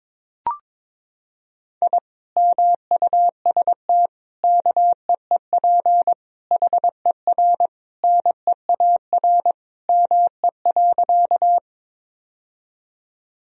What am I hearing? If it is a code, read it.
Morse code, heard as a keyed tone, about 22 wpm, 700 Hz. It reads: I MUST KEEP HER NEAR ME.
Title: Sentences from Top 1000 Words - Review Part 1 - 22wpm